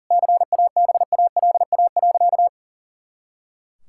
Morse code, heard as a keyed tone, about 40 wpm, 700 Hz. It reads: CABALA.
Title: Pangram Challenge v2 - 40wpm